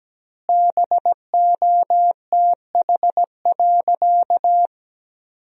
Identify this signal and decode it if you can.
Morse code, heard as a keyed tone, about 17 wpm, 700 Hz. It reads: BOTH.
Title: 4x Word-Spacing - Sentences from Top 300 Words - 17wpm